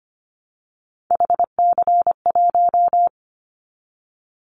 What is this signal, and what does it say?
Morse code, heard as a keyed tone, about 25 wpm, 700 Hz. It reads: HC1
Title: Callsign Prefixes for Top 100 DXCC - 25wpm